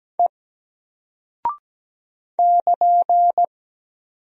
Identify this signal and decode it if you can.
Morse code, heard as a keyed tone, about 17 wpm, 700 Hz. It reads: E <KN>
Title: Single Character (Letter:Number:Punctuation:Prosign) Alt 17wpm